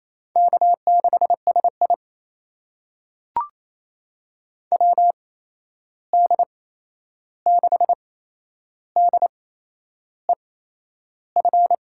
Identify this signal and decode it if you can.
Morse code, heard as a keyed tone, about 28 wpm, spaced as for 11 wpm, 700 Hz. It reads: K6SI WD6DEF
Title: USA Call Signs - 8x Character Spacing - 28wpm